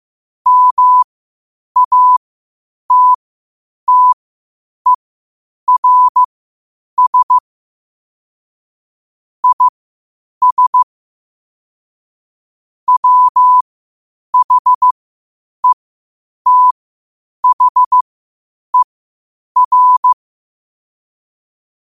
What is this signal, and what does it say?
Morse code, heard as a keyed tone, about 15 wpm, spaced as for 8 wpm, 1000 Hz. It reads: MATTERS IS WHETHER
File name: 8wpm